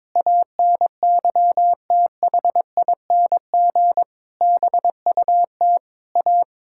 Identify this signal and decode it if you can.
Morse code, heard as a keyed tone, about 22 wpm, 700 Hz. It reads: ANYTHING BUT A
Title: Sentences from Top 1000 Words - Review Part 6 - 22wpm